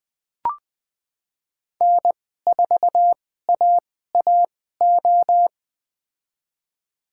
Morse code, heard as a keyed tone, about 20 wpm, spaced as for 14 wpm, 700 Hz. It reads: N4AAO
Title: USA Call Signs - 5 Characters - 2x Character Spacing - 20wpm